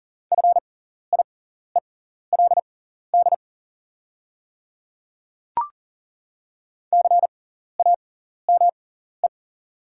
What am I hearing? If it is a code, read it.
Morse code, heard as a keyed tone, about 40 wpm, spaced as for 14 wpm, 690 Hz. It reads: FIELD CAME
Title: 6x Character Spacing - Top 500 Words - 40wpm